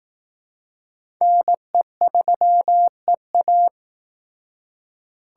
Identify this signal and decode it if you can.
Morse code, heard as a keyed tone, about 18 wpm, 700 Hz. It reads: NE3EA
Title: Speed Racing - USA Call Signs 15wpm